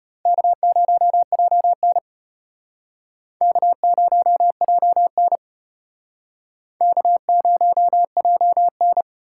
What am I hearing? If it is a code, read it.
Morse code, heard as a keyed tone, about 33 wpm, 700 Hz. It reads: K0JN K0JN K0JN